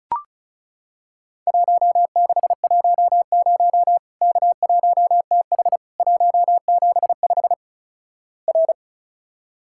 Morse code, heard as a keyed tone, about 36 wpm, 700 Hz. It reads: 1610 K1TH 175 R
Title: Worked All Europe DX Contest - QTC Exchange - 35wpm